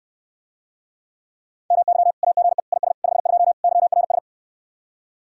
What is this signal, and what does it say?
Morse code, heard as a keyed tone, about 69 wpm, 700 Hz. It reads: MY AGE IS 52 YRS